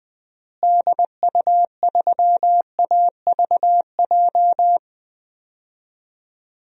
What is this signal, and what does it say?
Morse code, heard as a keyed tone, about 21 wpm, 700 Hz. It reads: DU3AVJ
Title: DXCC-100 Call Signs - 6 Characters - Repeated 2x - 20wpm